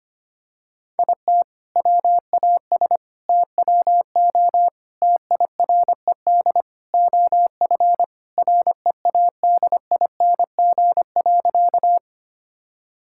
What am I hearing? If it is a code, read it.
Morse code, heard as a keyed tone, about 25 wpm, 700 Hz. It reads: IT WAS TWO TIRED OF READING.
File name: practice / 25wpm / All-Age Appropriate Jokes - 25wpm